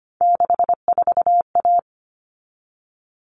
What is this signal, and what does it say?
Morse code, heard as a keyed tone, about 25 wpm, 700 Hz. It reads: 64A
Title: All Cut Numbers - 3-Digit 25wpm